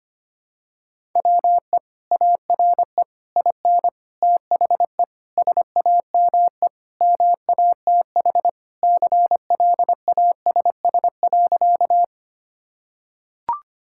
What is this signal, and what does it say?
Morse code, heard as a keyed tone, about 25 wpm, 700 Hz. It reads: WE ARE IN THE SAME MATH CLASS.
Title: Sentences from Top 1000 Words - Review Part 6 - 25wpm